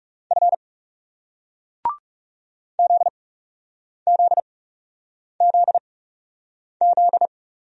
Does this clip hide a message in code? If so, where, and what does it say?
Yes: Morse code from 0.3 to 7.3 s, heard as a keyed tone, about 38 wpm, spaced as for 8 wpm, 700 Hz. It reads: F ZZZZ